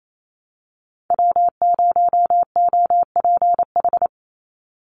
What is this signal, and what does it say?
Morse code, heard as a keyed tone, about 28 wpm, 700 Hz. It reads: W0OPH
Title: USA Call Signs 28wpm